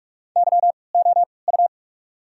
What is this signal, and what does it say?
Morse code, heard as a keyed tone, about 47 wpm, spaced as for 26 wpm, 700 Hz. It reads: YOU
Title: 3x Character Spacing - Top 300 Words - 45wpm